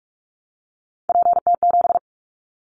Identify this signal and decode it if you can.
Morse code, heard as a keyed tone, about 44 wpm, 700 Hz. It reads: PT7
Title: Callsign Prefixes for Top 50 DXCC - 45wpm